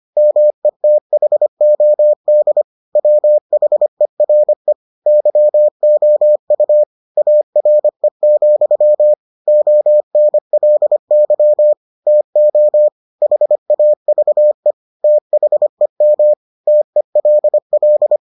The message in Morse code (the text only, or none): METHOD WHERE YOU ARE, ONLY TO HAVE THEM TELL